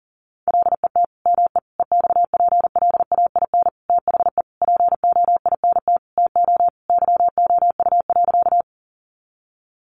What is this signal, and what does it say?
Morse code, heard as a keyed tone, about 40 wpm, 700 Hz. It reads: LET ME EXPLAIN THE POINT TO YOU.
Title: Sentences from Top 1100 Words 40wpm